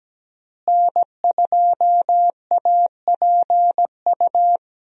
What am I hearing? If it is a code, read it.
Morse code, heard as a keyed tone, about 17 wpm, 700 Hz. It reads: N2APU
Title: USA Call Signs - 5 Characters - Repeated 2x - 17wpm